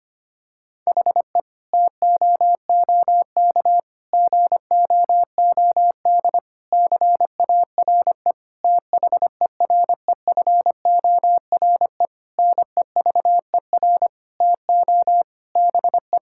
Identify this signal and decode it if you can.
Morse code, heard as a keyed tone, about 25 wpm, 700 Hz. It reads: HE TOOK GOOD CARE THEREFORE NEVER TO BE
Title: Sentences from Top 1000 Words - Review Part 5 - 25wpm